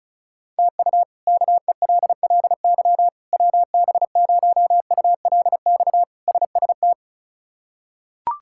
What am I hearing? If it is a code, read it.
Morse code, heard as a keyed tone, about 36 wpm, 700 Hz. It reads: TU KELLY WB0ULX SST
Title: K1USN SST QSO Practice - 35wpm